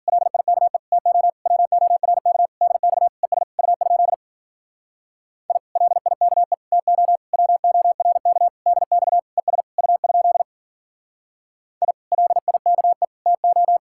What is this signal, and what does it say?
Morse code, heard as a keyed tone, about 47 wpm, 700 Hz. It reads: LIKE TO WORK DX ES U? I LIKE TO WORK DX ES U? I LIKE TO
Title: Speed-Racing Common QSO Phrases - 40wpm